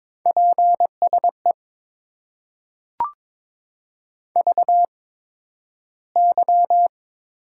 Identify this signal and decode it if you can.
Morse code, heard as a keyed tone, about 22 wpm, spaced as for 12 wpm, 700 Hz. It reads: PSE VY
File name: Top 90 QSO Elements - 8x Character Spacing - 22wpm